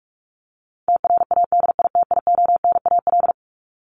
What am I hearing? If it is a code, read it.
Morse code, heard as a keyed tone, about 44 wpm, 700 Hz. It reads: TRADITIONAL
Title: Top 751-1000 Words 45wpm